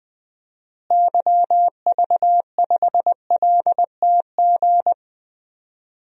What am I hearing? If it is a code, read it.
Morse code, heard as a keyed tone, about 20 wpm, 700 Hz. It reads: YV5LTG
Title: DXCC-100 Call Signs - 6 Characters - Repeated 3x - 20wpm